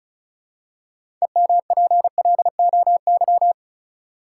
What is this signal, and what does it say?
Morse code, heard as a keyed tone, about 36 wpm, 700 Hz. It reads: EMPLOY
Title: Six Letter Words - 35wpm